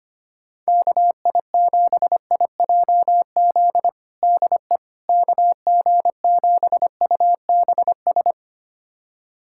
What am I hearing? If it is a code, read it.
Morse code, heard as a keyed tone, about 25 wpm, 700 Hz. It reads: KI7IJZ DE KG7UBS
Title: USA Callsign DE Callsign 25wpm